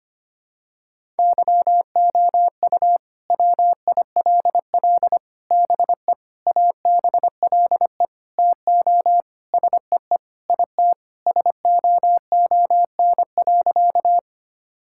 Morse code, heard as a keyed tone, about 25 wpm, 700 Hz. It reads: YOU WILL BE ABLE TO SEE IT SOON.
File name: practice / 25wpm / Sentences with 4-Word Phrases - 25wpm